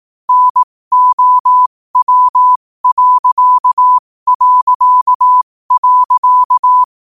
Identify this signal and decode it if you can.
Morse code, heard as a keyed tone, about 18 wpm, spaced as for 15 wpm, 1000 Hz. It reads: NOW...
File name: qotdfiles / 15wpm